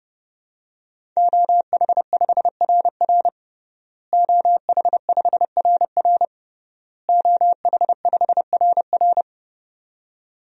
Morse code, heard as a keyed tone, about 30 wpm, 700 Hz. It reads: OH5RR OH5RR OH5RR
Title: DXCC-100 Call Signs - 5 Characters - Repeated 3x - 30wpm